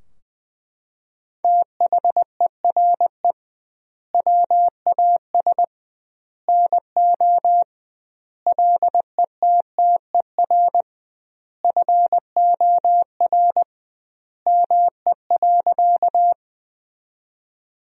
Morse code, heard as a keyed tone, about 20 wpm, 700 Hz. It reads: THERE WAS NO LETTER FOR ME.
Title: 2x Word-Spacing - Sentences from Top 300 Words - 20wpm